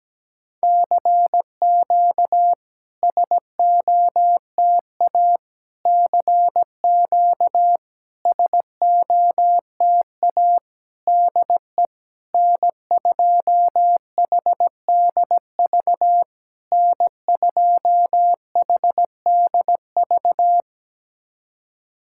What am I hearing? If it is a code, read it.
Morse code, heard as a keyed tone, about 17 wpm, 700 Hz. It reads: CQ SOTA CQ SOTA DE N2HDV N2HDV
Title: SOTA QSO Practice 17wpm